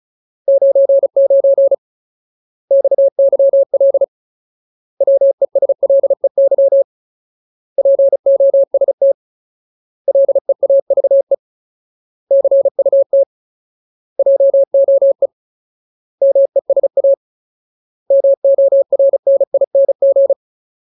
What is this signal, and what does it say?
Morse code, heard as a keyed tone, about 34 wpm, 550 Hz. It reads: 99 XYL WESLEY POST LEAVE CUT JOE MESA MORNING
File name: Ultimate Word List / 35WPM Ultimate